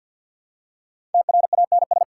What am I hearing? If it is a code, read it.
Morse code, heard as a keyed tone, about 53 wpm, 700 Hz. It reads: TRANS
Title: Prefixes and Suffixes 50wpm